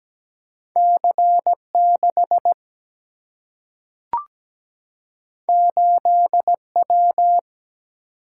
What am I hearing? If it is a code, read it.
Morse code, heard as a keyed tone, about 17 wpm, 700 Hz. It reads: C6 8W
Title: Single-Letter Single-Number 17wpm